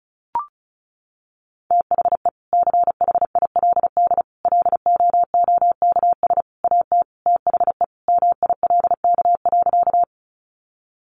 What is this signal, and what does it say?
Morse code, heard as a keyed tone, about 34 wpm, 700 Hz. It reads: THE CHILD LOOKS AT THE MILK.
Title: Additional Easy Phrases 35wpm